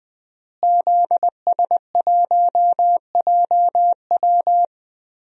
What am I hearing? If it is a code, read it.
Morse code, heard as a keyed tone, about 20 wpm, 700 Hz. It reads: ZS1JW
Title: DXCC-100 Call Signs - 5 Characters - Repeated 3x - 20wpm